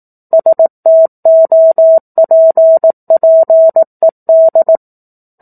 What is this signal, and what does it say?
Morse code, heard as a keyed tone, about 18 wpm, 650 Hz. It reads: STOPPED
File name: 04-Chapter-04 18 WPM